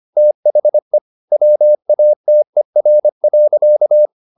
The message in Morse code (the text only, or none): THE WATER.